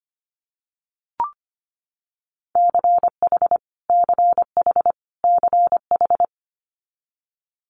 Morse code, heard as a keyed tone, about 25 wpm, 700 Hz. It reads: CH CH CH
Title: Multi-letter Phonemes Repeated 3x 25wpm